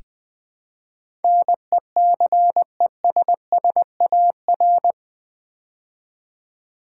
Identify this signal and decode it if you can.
Morse code, heard as a keyed tone, about 20 wpm, 700 Hz. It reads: NECESSAR
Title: Build A Word - 4 Letters At A Time - 11-Letter Words - 20wpm